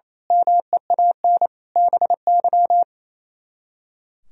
Morse code, heard as a keyed tone, about 29 wpm, 700 Hz. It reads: MEAN BY
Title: Speed-Racing - Sets of 5 Words 28wpm